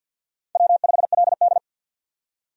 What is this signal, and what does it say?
Morse code, heard as a keyed tone, about 48 wpm, 700 Hz. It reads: W5LD
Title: USA Call Signs - 4 Characters - 50wpm